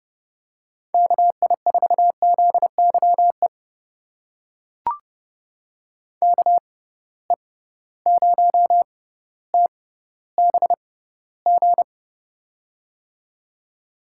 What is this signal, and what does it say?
Morse code, heard as a keyed tone, about 30 wpm, spaced as for 15 wpm, 700 Hz. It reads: KI4ZYE KE0TBG
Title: USA Call Signs - 6 Characters - 6x Character Spacing - 30wpm